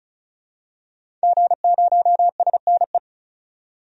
Morse code, heard as a keyed tone, about 36 wpm, 700 Hz. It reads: G0SNE
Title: DXCC-100 Call Signs - 5 Characters - 35wpm